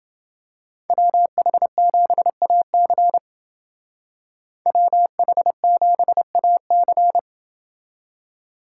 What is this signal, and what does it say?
Morse code, heard as a keyed tone, about 28 wpm, 700 Hz. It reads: WH7AC WH7AC